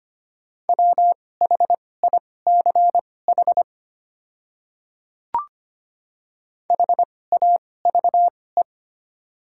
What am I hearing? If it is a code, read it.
Morse code, heard as a keyed tone, about 25 wpm, spaced as for 18 wpm, 700 Hz. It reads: WHICH HAVE
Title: 2x Character Spacing - Top 300 Words - 25wpm